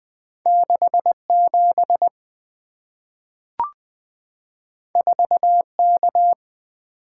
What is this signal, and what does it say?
Morse code, heard as a keyed tone, about 20 wpm, 690 Hz. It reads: 67 4K
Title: Two Letter-Number Combinations - 20wpm